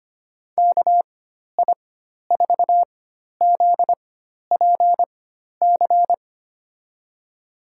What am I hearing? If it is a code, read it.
Morse code, heard as a keyed tone, about 25 wpm, spaced as for 12 wpm, 700 Hz. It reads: KI4ZPC
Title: USA Call Signs - 6 Characters - 4x Character Spacing - 25wpm